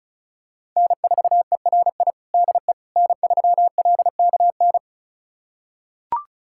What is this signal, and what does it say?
Morse code, heard as a keyed tone, about 36 wpm, 700 Hz. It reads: N4ERI DE N3LKN
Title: USA Callsign DE Callsign - 5 Characters - 35wpm